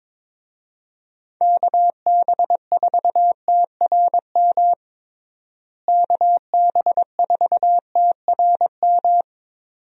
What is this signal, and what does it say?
Morse code, heard as a keyed tone, about 22 wpm, 700 Hz. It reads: KB4TRM KB4TRM
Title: USA Call Signs - 6 Characters - Repeated 2x - 22wpm